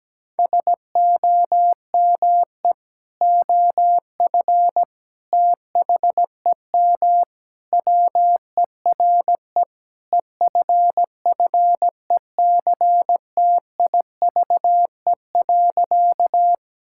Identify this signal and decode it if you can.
Morse code, heard as a keyed tone, about 17 wpm, 700 Hz. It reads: SOME OF THEM WERE EFFECTIVE.
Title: Sentences from Top 1000 Words - Review Part 6 - 17wpm